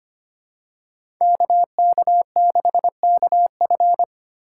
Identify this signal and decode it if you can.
Morse code, heard as a keyed tone, about 25 wpm, 700 Hz. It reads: KK6KF
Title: USA Call Signs - 5 Characters - 2x Character Spacing - 25wpm